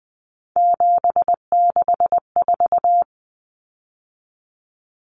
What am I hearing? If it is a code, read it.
Morse code, heard as a keyed tone, about 20 wpm, 700 Hz. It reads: 764
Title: Numbers - 3-Digit 20wpm